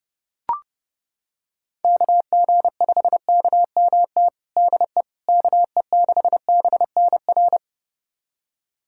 Morse code, heard as a keyed tone, about 30 wpm, 700 Hz. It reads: KG5KMT DE KE6BNR
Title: USA Callsign DE Callsign - 6 Characters - 30wpm